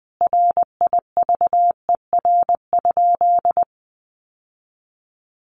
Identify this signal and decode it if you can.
Morse code, heard as a keyed tone, about 20 wpm, 700 Hz. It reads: RIVER?
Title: Sentences from Top 1400 Words 20wpm